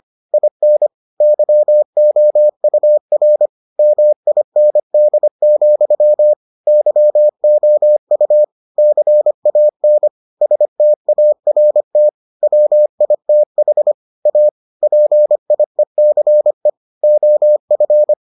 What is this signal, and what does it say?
Morse code, heard as a keyed tone, about 25 wpm, 600 Hz. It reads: IN YOUR MIND, YOU CAN START WITH A PIECE OF